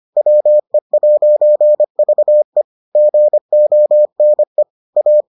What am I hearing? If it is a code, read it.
Morse code, heard as a keyed tone, about 25 wpm, 600 Hz. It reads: WE'VE GONE A